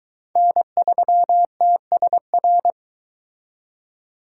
Morse code, heard as a keyed tone, about 21 wpm, 700 Hz. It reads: N3TSR